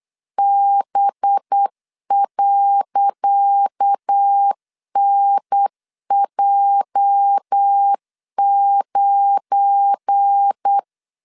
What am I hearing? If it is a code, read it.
Morse code, heard as a keyed tone, about 8 wpm, 800 Hz. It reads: B.NJ9
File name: LEZIONI / 10 SIGNS / 10D COMA 8wpm 3749